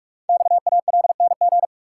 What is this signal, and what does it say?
Morse code, heard as a keyed tone, about 44 wpm, 700 Hz. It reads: XALNG
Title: Code Groups - 5 Letters - 45wpm